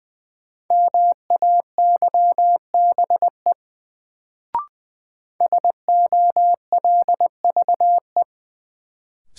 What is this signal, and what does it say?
Morse code, heard as a keyed tone, about 20 wpm, 700 Hz. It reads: MAYBE SOLVE
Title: Sentences from Top 1000 Words - New Word Drill - 20wpm